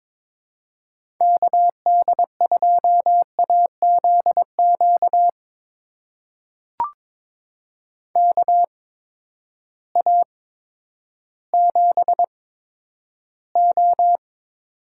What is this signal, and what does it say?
Morse code, heard as a keyed tone, about 22 wpm, spaced as for 11 wpm, 700 Hz. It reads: KD2AZQ KA7O
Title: USA Call Signs - 8x Character Spacing - 22wpm